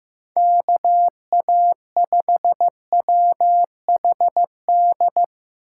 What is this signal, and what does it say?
Morse code, heard as a keyed tone, about 15 wpm, 700 Hz. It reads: KA5WHD
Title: USA Call Signs - 6 Characters - 3x Character Spacing - 15wpm